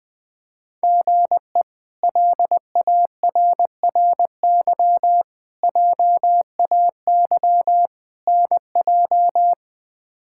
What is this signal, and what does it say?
Morse code, heard as a keyed tone, about 21 wpm, 700 Hz. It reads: GE LARRY JAY NJ
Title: K1USN SST QSO Practice - 20wpm